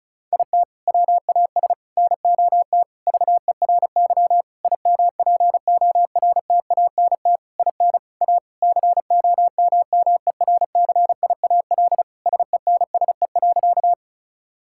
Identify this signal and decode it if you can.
Morse code, heard as a keyed tone, about 36 wpm, 700 Hz. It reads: IT WAS NOT VERY IMPORTANT IN A COMMERCIAL SENSE.
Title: Sentences from Top 1000 Words - Review Part 5 - 35wpm